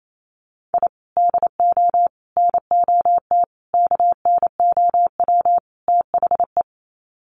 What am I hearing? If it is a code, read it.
Morse code, heard as a keyed tone, about 28 wpm, 700 Hz. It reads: I DO NOT KNOW THE
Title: Sets of 5 Words 28wpm